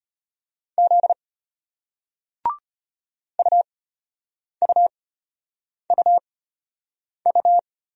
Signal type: Morse code

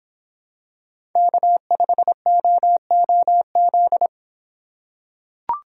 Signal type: Morse code

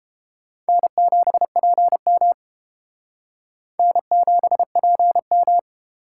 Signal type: Morse code

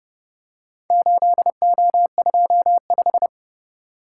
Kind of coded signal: Morse code

